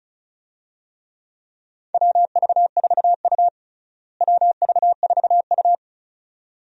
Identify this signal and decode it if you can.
Morse code, heard as a keyed tone, about 36 wpm, 700 Hz. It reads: WV4U WV4U
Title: USA Call Signs - 4 Characters - Repeated 2x - 35wpm